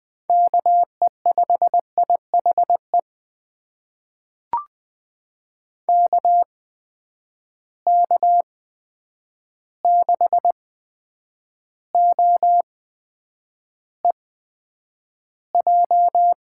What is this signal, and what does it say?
Morse code, heard as a keyed tone, about 20 wpm, spaced as for 9 wpm, 700 Hz. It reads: KE5IHE KK6OEJ
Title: USA Call Signs - 6 Characters - 8x Character Spacing - 20wpm